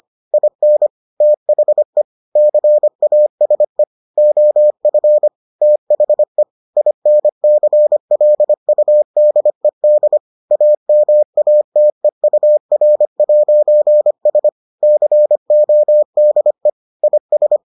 Morse code, heard as a keyed tone, about 25 wpm, 600 Hz. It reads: IN THE CASE OF THE INCLUDED AMATEUR'S CODE IS